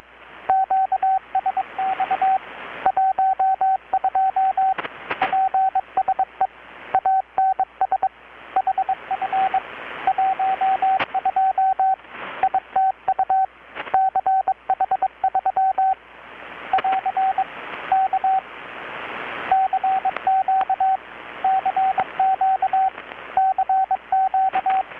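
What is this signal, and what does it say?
Morse code, heard as a keyed tone, about 21 wpm, 760 Hz, with radio noise. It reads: QSX 12 PSE ANS HF 12 ITU CH3 <AR> K CQ CQ CQ
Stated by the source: Morse from KPH's Night of Nights XX maritime radiotelegraphy event, received in Chiapas, Mexico about 3,800 km away on a Kenwood R-600 with a grounded 10 m long-wire antenna.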